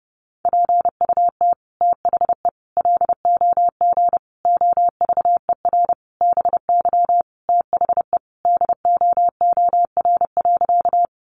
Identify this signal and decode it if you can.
Morse code, heard as a keyed tone, about 30 wpm, 700 Hz. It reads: PUT THE LOG OVER BY THE DOOR.
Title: Sentences from Top 1400 Words 30wpm